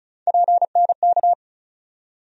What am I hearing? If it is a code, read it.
Morse code, heard as a keyed tone, about 36 wpm, 700 Hz. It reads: PNK
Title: USA Call Sign Postfixes - 3 Letters 35wpm